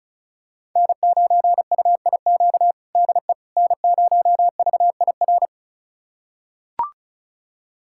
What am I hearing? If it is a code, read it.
Morse code, heard as a keyed tone, about 36 wpm, 700 Hz. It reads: N9UIQ DE N0VIR
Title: USA Callsign DE Callsign - 5 Characters - 35wpm